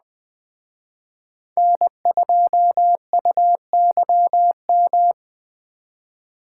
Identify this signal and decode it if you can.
Morse code, heard as a keyed tone, about 20 wpm, 700 Hz. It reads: N2UYM